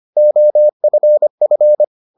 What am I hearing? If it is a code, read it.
Morse code, heard as a keyed tone, about 25 wpm, 600 Hz. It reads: OFF